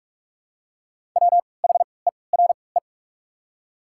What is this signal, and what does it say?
Morse code, heard as a keyed tone, about 47 wpm, spaced as for 26 wpm, 700 Hz. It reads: WHERE
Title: 3x Character Spacing - Top 300 Words - 45wpm